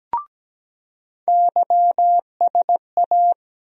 Morse code, heard as a keyed tone, about 17 wpm, 700 Hz. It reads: YSA
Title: USA Call Sign Postfixes - 3 Letters 17wpm